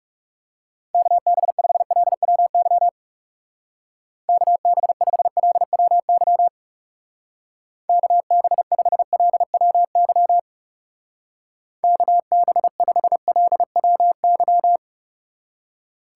Morse code, heard as a keyed tone, about 35 wpm, 700 Hz. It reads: KB5LWY KB5LWY KB5LWY KB5LWY
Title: Speed Racing - USA Call Signs 30wpm